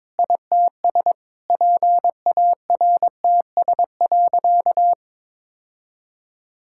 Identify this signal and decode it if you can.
Morse code, heard as a keyed tone, about 22 wpm, 700 Hz. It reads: ITS PARTS.
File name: Sentences with 7-Word Phrases - 22wpm